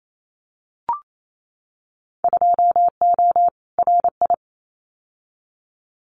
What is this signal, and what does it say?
Morse code, heard as a keyed tone, about 28 wpm, 700 Hz. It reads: 2O RI
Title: Winter Field Day Exchanges 28wpm